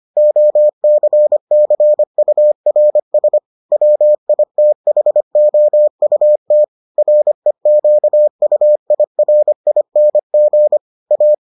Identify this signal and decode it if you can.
Morse code, heard as a keyed tone, about 25 wpm, 600 Hz. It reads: OCCURS WITHOUT REQUIRING A